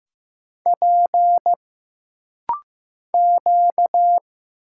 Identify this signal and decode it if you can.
Morse code, heard as a keyed tone, about 15 wpm, 700 Hz. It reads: P Q
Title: Single Letter-Number Alt - Rapid-Fire 15wpm